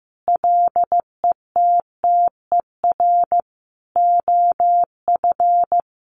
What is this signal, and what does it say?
Morse code, heard as a keyed tone, about 15 wpm, 700 Hz. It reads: LETTER OF
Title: Sentences from Top 1400 Words 15wpm